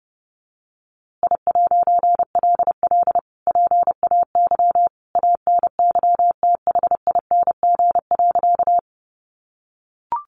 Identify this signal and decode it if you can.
Morse code, heard as a keyed tone, about 30 wpm, 700 Hz. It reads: I'LL PAY ANYTHING.
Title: A Sound Of Thunder 30wpm